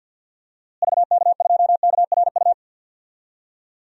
Morse code, heard as a keyed tone, about 47 wpm, 700 Hz. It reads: VK2KRU
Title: DXCC-100 Call Signs - 6 Characters - Repeated 2x - 50wpm